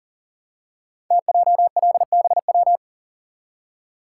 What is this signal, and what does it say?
Morse code, heard as a keyed tone, about 39 wpm, 700 Hz. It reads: TJLBW
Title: Code Groups - 5 Letter:Numbers - 40wpm